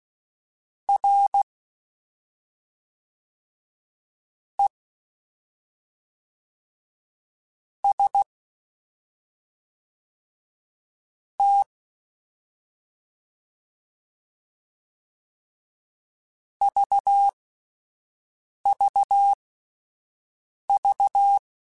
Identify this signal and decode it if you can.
Morse code, heard as a keyed tone, about 16 wpm, spaced as for 5 wpm, 800 Hz. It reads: R E S T VVV